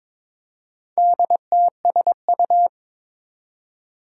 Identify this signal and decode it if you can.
Morse code, heard as a keyed tone, about 22 wpm, 700 Hz. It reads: DTSU
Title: Four Challenging Letter-Number Combinations 22wpm